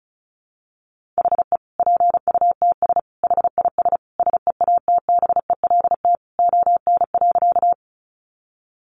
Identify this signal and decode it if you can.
Morse code, heard as a keyed tone, about 34 wpm, 700 Hz. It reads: HE PUTS HIS SEATBELT ON.
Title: Additional Easy Phrases 35wpm